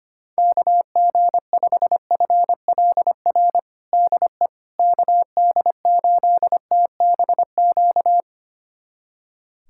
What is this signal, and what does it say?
Morse code, heard as a keyed tone, about 25 wpm, 700 Hz. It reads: KG5FLR DE KD8TBQ
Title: USA Callsign DE Callsign 25wpm